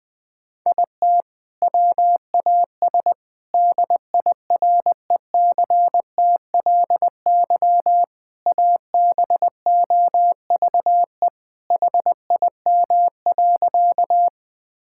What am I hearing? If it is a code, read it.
Morse code, heard as a keyed tone, about 21 wpm, 700 Hz. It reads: IT WAS DIRECTLY ABOVE HIM.
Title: Sentences from Top 1000 Words - Review Part 5 - 20wpm